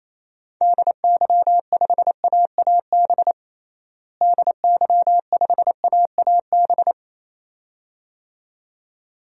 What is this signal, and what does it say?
Morse code, heard as a keyed tone, about 27 wpm, 700 Hz. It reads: DY5AAB DY5AAB
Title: DXCC-100 Call Signs - 6 Characters - Repeated 2x - 28wpm